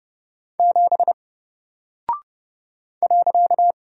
Morse code, heard as a keyed tone, about 30 wpm, 700 Hz. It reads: 7 .
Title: Single Character (Letter:Number:Punctuation:Prosign) Alt 30wpm